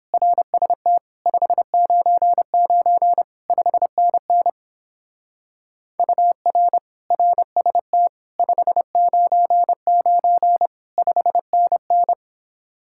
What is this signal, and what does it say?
Morse code, heard as a keyed tone, about 29 wpm, 700 Hz. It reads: RST 599 5NN UR RST 599 5NN